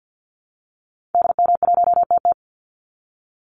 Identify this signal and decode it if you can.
Morse code, heard as a keyed tone, about 50 wpm, 700 Hz. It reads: DM1TT
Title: Call Signs Repeated 3x - 50wpm